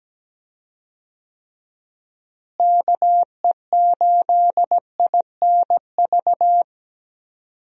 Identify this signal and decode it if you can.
Morse code, heard as a keyed tone, about 17 wpm, 700 Hz. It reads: KE8INV
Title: USA Call Signs - 6 Characters - Repeated 3x - 17wpm